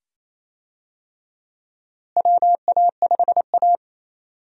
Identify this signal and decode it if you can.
Morse code, heard as a keyed tone, about 28 wpm, 700 Hz. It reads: WA5A
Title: USA Call Signs - 4 Characters - Repeated 2x - 28wpm